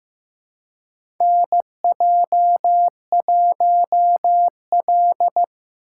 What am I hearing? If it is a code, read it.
Morse code, heard as a keyed tone, about 15 wpm, 700 Hz. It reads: NJ1L
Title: USA Call Signs - 4 Characters - 3x Character Spacing - 15wpm